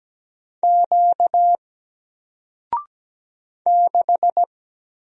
Morse code, heard as a keyed tone, about 17 wpm, 700 Hz. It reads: Q 6
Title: Single Letter-Number 17wpm